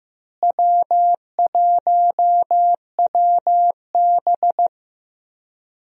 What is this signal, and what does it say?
Morse code, heard as a keyed tone, about 15 wpm, 700 Hz. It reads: W1WB
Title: USA Call Signs - 4 Characters - Repeated 3x - 15wpm